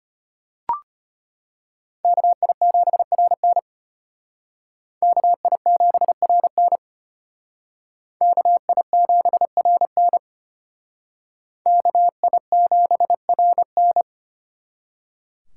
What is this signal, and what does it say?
Morse code, heard as a keyed tone, about 32 wpm, 700 Hz. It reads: KI7RN KI7RN KI7RN KI7RN